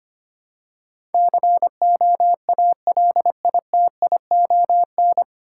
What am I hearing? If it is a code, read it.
Morse code, heard as a keyed tone, about 25 wpm, 690 Hz. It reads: COALITION
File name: practice / 25wpm / 9 Letter Words - First 3 Letters Given - 25wpm